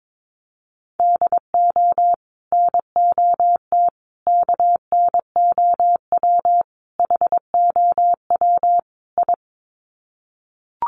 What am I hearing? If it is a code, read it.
Morse code, heard as a keyed tone, about 22 wpm, 700 Hz. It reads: DO NOT KNOW HOW I
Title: Sets of 5 Words Encore2 22wpm